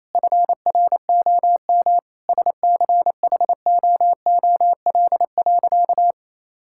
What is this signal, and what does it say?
Morse code, heard as a keyed tone, about 28 wpm, 700 Hz. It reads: FROM SCHOOL.
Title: Sentences from Top 1000 Words - Review Part 6 - 28wpm